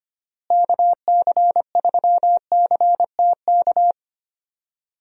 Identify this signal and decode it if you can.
Morse code, heard as a keyed tone, about 25 wpm, 700 Hz. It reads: KC3CTK
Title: USA Call Signs - 6 Characters - Repeated 3x - 25wpm